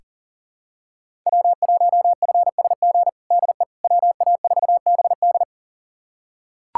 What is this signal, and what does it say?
Morse code, heard as a keyed tone, about 40 wpm, 700 Hz. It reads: W1FSG DE WA4BD
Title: USA Callsign DE Callsign - 5 Characters - 40wpm